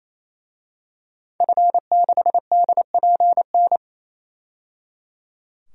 Morse code, heard as a keyed tone, about 28 wpm, 700 Hz. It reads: F6DPN
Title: DXCC-100 Call Signs - 5 Characters - 28wpm